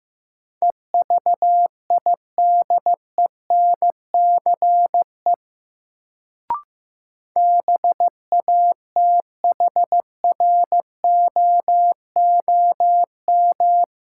Morse code, heard as a keyed tone, about 15 wpm, 700 Hz. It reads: EVIDENCE BATHROOM
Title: Sets of 5 Words Encore - New Word Drill - 15wpm